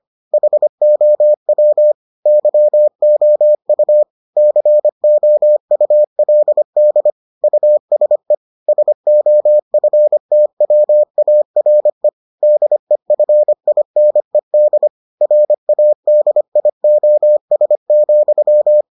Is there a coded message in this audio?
Yes: Morse code from 0.3 to 18.8 s, heard as a keyed tone, about 25 wpm, 600 Hz. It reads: HOW YOU COULD USE SOFTWARE DEFINED RADIOS,